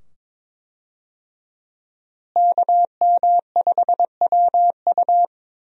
Morse code, heard as a keyed tone, about 22 wpm, 700 Hz. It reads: KM5WU
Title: USA Call Signs - 5 Characters - Repeated 2x - 22wpm